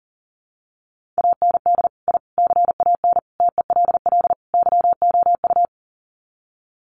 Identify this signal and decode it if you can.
Morse code, heard as a keyed tone, about 40 wpm, 700 Hz. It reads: AND I CAN TELL YOU
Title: Sets of 5 Words Encore2 40wpm